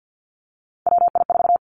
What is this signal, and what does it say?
Morse code, heard as a keyed tone, about 50 wpm, 700 Hz. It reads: WI4
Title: Three Challenging Letter-Number Combinations 50wpm